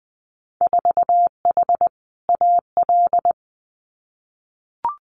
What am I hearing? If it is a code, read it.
Morse code, heard as a keyed tone, about 20 wpm, 700 Hz. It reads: 4H AL
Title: Winter Field Day Exchanges Encore 20wpm